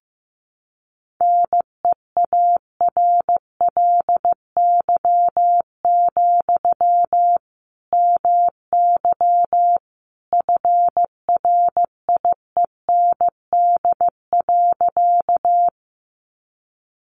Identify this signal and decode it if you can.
Morse code, heard as a keyed tone, about 15 wpm, 700 Hz. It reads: NEARLY, MY FRIEND.
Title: Sentences from Top 1300 Words 15wpm